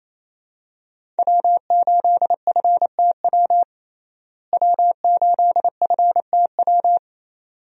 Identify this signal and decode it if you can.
Morse code, heard as a keyed tone, about 28 wpm, 700 Hz. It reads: W8FTW W8FTW
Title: USA Call Signs - 5 Characters - Repeated 2x - 28wpm